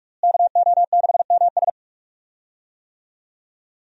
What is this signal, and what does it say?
Morse code, heard as a keyed tone, about 42 wpm, 700 Hz. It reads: KO6MS
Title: USA Call Signs - 5 Characters - Repeated 2x - 45wpm